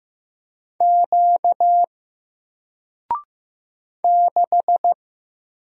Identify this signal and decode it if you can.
Morse code, heard as a keyed tone, about 15 wpm, 700 Hz. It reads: Q 6
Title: Single Letter-Number Alt 15wpm